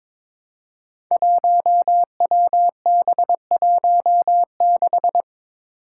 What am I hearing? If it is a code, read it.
Morse code, heard as a keyed tone, about 22 wpm, 700 Hz. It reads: 1WB16
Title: Five Challenging Letter-Number Combinations 22wpm